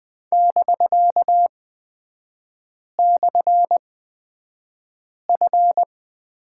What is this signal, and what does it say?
Morse code, heard as a keyed tone, about 20 wpm, spaced as for 5 wpm, 700 Hz. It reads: <BK>/F
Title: 250 Lesson 40e - BK - Three Characters 20wpm